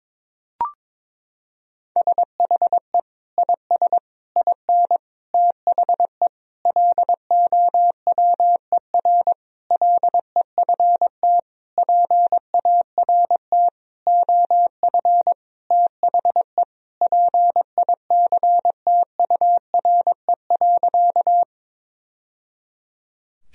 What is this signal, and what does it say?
Morse code, heard as a keyed tone, about 22 wpm, 700 Hz. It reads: SHE IS IN THE LOWER LEFT PART OF THE PICTURE.
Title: Sentences from Top 1000 Words - Review Part 4 - 22wpm